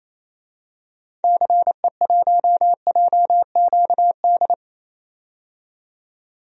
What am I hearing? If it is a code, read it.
Morse code, heard as a keyed tone, about 28 wpm, 700 Hz. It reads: CE1JQD
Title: DXCC-100 Call Signs - 6 Characters - 28wpm